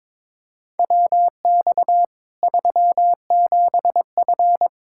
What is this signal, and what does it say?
Morse code, heard as a keyed tone, about 22 wpm, 700 Hz. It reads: WX 37F
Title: Practice QSO - Exchange 2 - Info - 22wpm